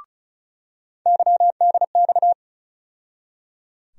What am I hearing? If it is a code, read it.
Morse code, heard as a keyed tone, about 36 wpm, 700 Hz. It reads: YDX
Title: USA Call Sign Postfixes - 3 Letters 35wpm